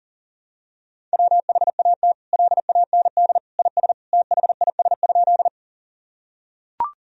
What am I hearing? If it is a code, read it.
Morse code, heard as a keyed tone, about 40 wpm, 700 Hz. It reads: WHAT LAND IS THIS?
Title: Sentences from Top 1000 Words - Review Part 1 - 40wpm